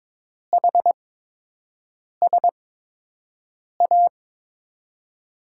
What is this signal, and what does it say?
Morse code, heard as a keyed tone, about 23 wpm, spaced as for 6 wpm, 700 Hz. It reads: HSA
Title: USA Call Signs - 6 Characters - 8x Character Spacing - 22wpm